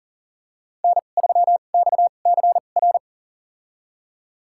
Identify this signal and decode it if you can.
Morse code, heard as a keyed tone, about 37 wpm, spaced as for 28 wpm, 700 Hz. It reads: N3XCR
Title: USA Call Signs - 5 Characters - 2x Character Spacing - 40wpm